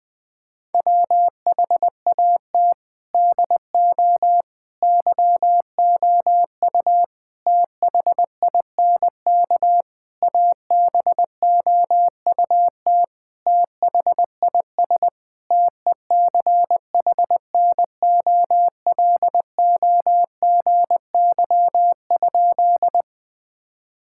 Morse code, heard as a keyed tone, about 20 wpm, 700 Hz. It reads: WHAT DO YOU THINK ABOUT THIS TECHNOLOGY?
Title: Sentences from Top 1000 Words - Review Part 5 - 20wpm